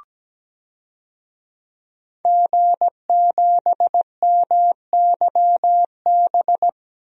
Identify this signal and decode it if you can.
Morse code, heard as a keyed tone, about 17 wpm, 700 Hz. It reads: G7MYB
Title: DXCC-100 Call Signs - 5 Characters - 17wpm